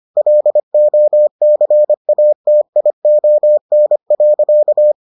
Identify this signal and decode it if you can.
Morse code, heard as a keyed tone, about 25 wpm, 600 Hz. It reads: LOCATION.